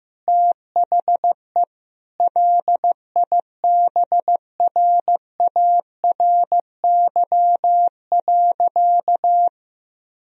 Morse code, heard as a keyed tone, about 15 wpm, 700 Hz. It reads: THE LIBRARY.
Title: Sentences from Top 1000 Words - Review Part 3 - 15wpm